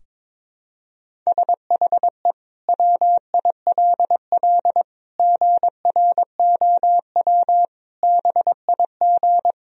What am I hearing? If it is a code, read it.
Morse code, heard as a keyed tone, about 22 wpm, 700 Hz. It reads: SHE WILL GROW BIG
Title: Sentences from Top 1000 Words - Review Part 1 - 22wpm